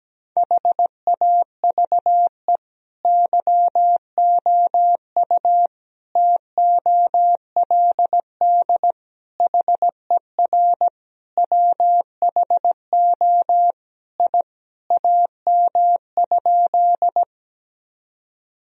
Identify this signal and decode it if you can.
Morse code, heard as a keyed tone, about 17 wpm, 700 Hz. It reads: HAVE YOU TOLD HER WHO I AM?
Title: Sentences from Top 1000 Words - Review Part 2 - 17wpm